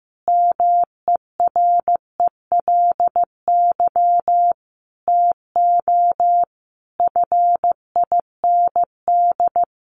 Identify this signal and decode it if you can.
Morse code, heard as a keyed tone, about 15 wpm, 700 Hz. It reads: MERELY TO FIND
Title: Sentences from Top 1400 Words 15wpm